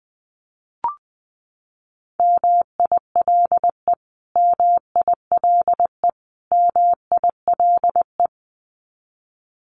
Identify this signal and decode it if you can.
Morse code, heard as a keyed tone, about 19 wpm, 700 Hz. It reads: MILE MILE MILE
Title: Top 500 Words Repeated 20wpm